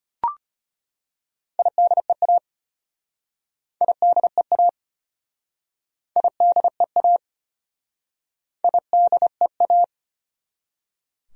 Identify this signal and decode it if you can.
Morse code, heard as a keyed tone, about 33 wpm, 700 Hz. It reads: IDEA IDEA IDEA IDEA